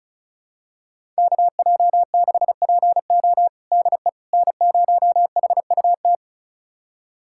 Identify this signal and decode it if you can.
Morse code, heard as a keyed tone, about 36 wpm, 700 Hz. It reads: KJ6PO DE N0HUT
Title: USA Callsign DE Callsign - 5 Characters - 35wpm